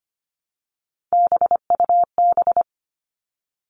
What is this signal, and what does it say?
Morse code, heard as a keyed tone, about 25 wpm, 700 Hz. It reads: BUB
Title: All Cut Numbers - 3-Digit 25wpm